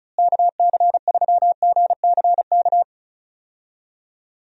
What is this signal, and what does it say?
Morse code, heard as a keyed tone, about 34 wpm, 700 Hz. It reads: KC3GCK
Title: USA Call Signs - 6 Characters - Repeated 3x - 35wpm